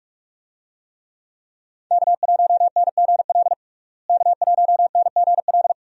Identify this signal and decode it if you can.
Morse code, heard as a keyed tone, about 46 wpm, 700 Hz. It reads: K1NGL K1NGL
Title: USA Call Signs - 5 Characters - Repeated 3x - 45wpm